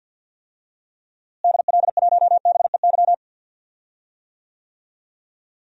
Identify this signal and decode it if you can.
Morse code, heard as a keyed tone, about 49 wpm, 700 Hz. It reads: DL1BEY
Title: DXCC-100 Call Signs - 6 Characters - Repeated 2x - 50wpm